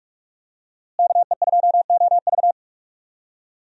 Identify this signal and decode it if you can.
Morse code, heard as a keyed tone, about 46 wpm, 700 Hz. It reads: KE2OV
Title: USA Call Signs - 5 Characters - Repeated 3x - 45wpm